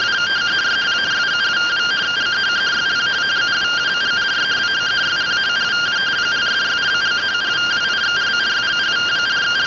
Two-tone teleprinter signal, about 170 Hz shift.
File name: findings / CT 205 ax mips, iDose 3 0006.gen-40 rtty rx